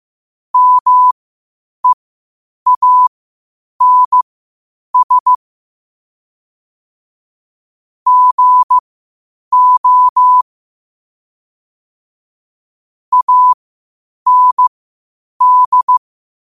Morse code, heard as a keyed tone, about 15 wpm, spaced as for 8 wpm, 1000 Hz. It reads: MEANS GO AND
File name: qotdfiles / 8wpm